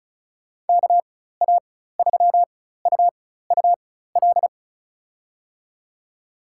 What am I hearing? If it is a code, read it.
Morse code, heard as a keyed tone, about 34 wpm, spaced as for 16 wpm, 700 Hz. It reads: KA3UUL
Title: USA Call Signs - 6 Characters - 4x Character Spacing - 35wpm